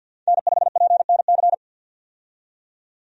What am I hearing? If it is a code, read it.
Morse code, heard as a keyed tone, about 53 wpm, 700 Hz. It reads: N5PNC
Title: USA Call Signs - 5 Characters - 50wpm